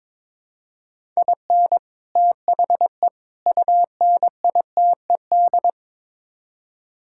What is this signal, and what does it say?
Morse code, heard as a keyed tone, about 21 wpm, 700 Hz. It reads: IN THE UNITED
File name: practice / 22wpm / Build A Phrase - 5-Word Phrases - 22wpm